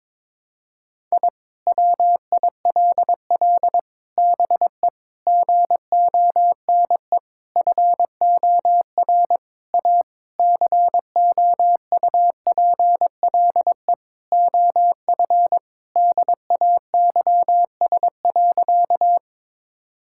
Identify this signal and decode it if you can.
Morse code, heard as a keyed tone, about 22 wpm, 700 Hz. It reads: I WILL BE GONE FOR A COUPLE OF DAYS.
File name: practice / 22wpm / Sentences with 4-Word Phrases - 22wpm